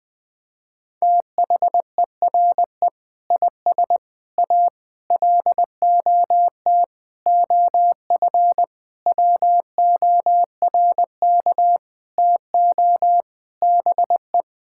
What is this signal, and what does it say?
Morse code, heard as a keyed tone, about 21 wpm, 700 Hz. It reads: THERE IS A LOT OF WORK TO BE
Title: Sentences with 7-Word Phrases - 20wpm